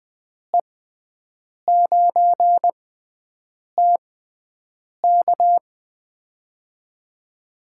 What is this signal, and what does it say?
Morse code, heard as a keyed tone, about 21 wpm, spaced as for 7 wpm, 700 Hz. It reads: E9TK
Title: USA Call Signs - 5 Characters - 6x Character Spacing - 20wpm